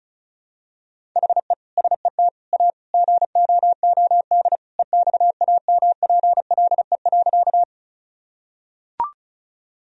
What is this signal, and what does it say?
Morse code, heard as a keyed tone, about 36 wpm, 700 Hz. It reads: HE SET A GOOD EXAMPLE.
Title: Sentences from Top 1000 Words - Review Part 1 - 35wpm